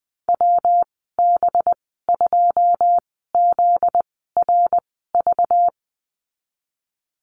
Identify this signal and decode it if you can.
Morse code, heard as a keyed tone, about 20 wpm, spaced as for 14 wpm, 700 Hz. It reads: WB2ZRV
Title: 245 Lesson 39g - Letter z - Call Signs 20wpm